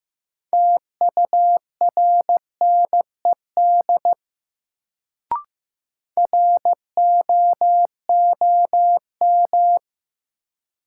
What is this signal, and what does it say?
Morse code, heard as a keyed tone, about 15 wpm, 700 Hz. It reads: TURNED ROOM
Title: Sets of 4 Words - New Word Drill - 15wpm